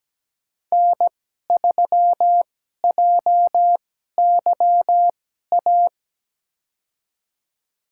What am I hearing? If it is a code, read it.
Morse code, heard as a keyed tone, about 17 wpm, spaced as for 12 wpm, 700 Hz. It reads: N3JYA
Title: USA Call Signs - 2x Character Spacing - 17wpm